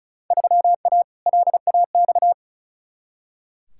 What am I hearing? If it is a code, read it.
Morse code, heard as a keyed tone, about 36 wpm, 700 Hz. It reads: 3A LAX
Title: Field Day Exchanges - 35wpm